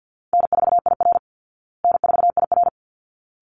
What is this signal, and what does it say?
Morse code, heard as a keyed tone, about 50 wpm, 700 Hz. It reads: N4IR N4IR
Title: Call Signs Repeated 3x - 50wpm